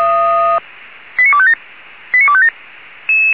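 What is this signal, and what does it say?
Five-tone selective call (ZVEI), heard as quick runs of five tones, tones of about 70 ms. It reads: 89257 89257
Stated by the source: ZVEI five-tone selective call received by SDR.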